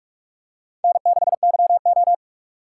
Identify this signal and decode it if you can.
Morse code, heard as a keyed tone, about 45 wpm, 700 Hz. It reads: N6YO
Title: USA Call Signs - 4 Characters - 6x Character Spacing - 45wpm